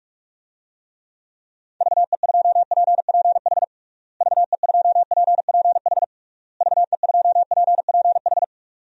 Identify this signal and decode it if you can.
Morse code, heard as a keyed tone, about 46 wpm, 700 Hz. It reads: VE2PPH VE2PPH VE2PPH
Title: DXCC-100 Call Signs - 6 Characters - Repeated 3x - 45wpm